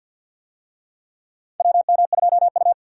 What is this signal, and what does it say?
Morse code, heard as a keyed tone, about 53 wpm, 700 Hz. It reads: WM2U
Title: USA Call Signs - 4 Characters - 50wpm